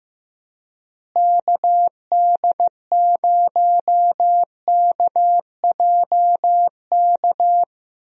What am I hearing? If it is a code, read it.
Morse code, heard as a keyed tone, about 15 wpm, 700 Hz. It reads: KD0KJK
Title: USA Call Signs - 6 Characters - Repeated 3x - 15wpm